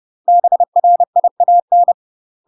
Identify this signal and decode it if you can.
Morse code, heard as a keyed tone, about 30 wpm, 700 Hz. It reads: BRIAN
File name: UK MASTER QSOs TRAINING / 30WPM / UK MASTER QSOs 0015